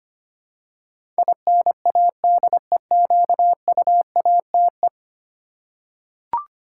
Morse code, heard as a keyed tone, about 25 wpm, 700 Hz. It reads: INADEQUATE
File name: Ten Letter Words - 25wpm